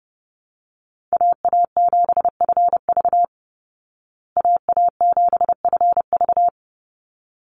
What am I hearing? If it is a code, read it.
Morse code, heard as a keyed tone, about 30 wpm, 700 Hz. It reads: AA7FV AA7FV
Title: Call Signs Repeated 3x - 30wpm